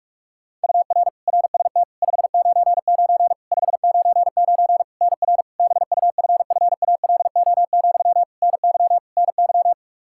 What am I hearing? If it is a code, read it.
Morse code, heard as a keyed tone, about 46 wpm, 700 Hz. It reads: UR RST 599 599 NR BUFFALO, NY NY
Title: K1USN SST QSO Practice 45wpm